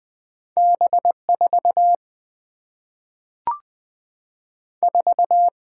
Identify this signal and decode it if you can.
Morse code, heard as a keyed tone, about 20 wpm, 700 Hz. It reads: B 4 4
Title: Single-Letter Single-Number 20wpm